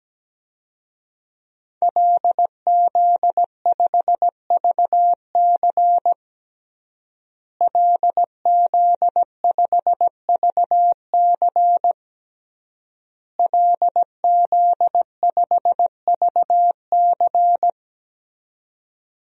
Morse code, heard as a keyed tone, about 17 wpm, 700 Hz. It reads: LZ5VC LZ5VC LZ5VC
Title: DXCC-100 Call Signs - 5 Characters - Repeated 3x - 17wpm